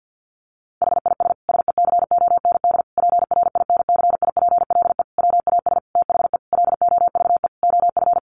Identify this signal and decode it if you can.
Morse code, heard as a keyed tone, about 49 wpm, 700 Hz. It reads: HIS SECOND PRINCIPLE WAS THE LOVE OF
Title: Sentences from Top 1400 Words 50wpm